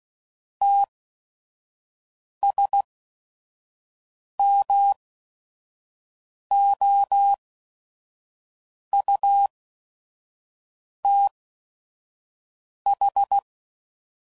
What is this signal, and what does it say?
Morse code, heard as a keyed tone, about 16 wpm, spaced as for 5 wpm, 800 Hz. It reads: TSMOUTH
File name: CW Code Groups @5WPM / fromSM's / Cities